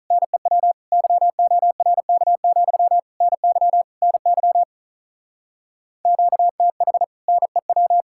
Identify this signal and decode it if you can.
Morse code, heard as a keyed tone, about 43 wpm, 700 Hz. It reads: NEW YORK, NY NY QTH NEW
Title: Speed-Racing Common QSO Phrases - 35wpm